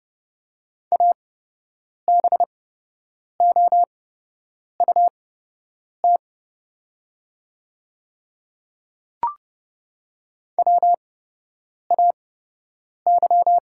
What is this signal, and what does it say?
Morse code, heard as a keyed tone, about 30 wpm, spaced as for 8 wpm, 700 Hz. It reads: ABOUT WAY
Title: 8x Character Spacing - Top 100 Words - 30wpm